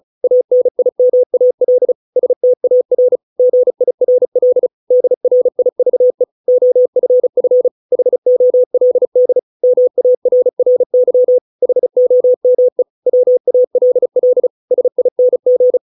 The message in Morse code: ANIMAL STAR GIRL DRIVE OFF HOLD MARRY HOME WALL SING

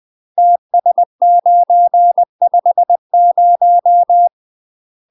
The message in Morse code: TS950

